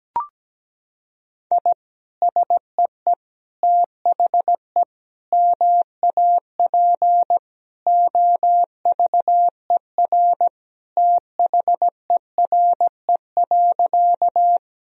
Morse code, heard as a keyed tone, about 17 wpm, 700 Hz. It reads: I SEE THE MAP OVER THERE.
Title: Sentences from Top 1000 Words - Review Part 2 - 17wpm